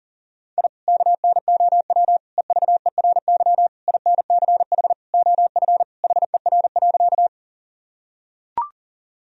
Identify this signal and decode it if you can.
Morse code, heard as a keyed tone, about 40 wpm, 700 Hz. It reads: I KNOW EVERY INCH OF HER.
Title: Sentences from Top 1000 Words - Review Part 3 - 40wpm